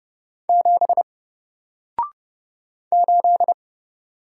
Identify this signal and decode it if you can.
Morse code, heard as a keyed tone, about 30 wpm, 700 Hz. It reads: 7 8
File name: Single-Letter-Number-30wpm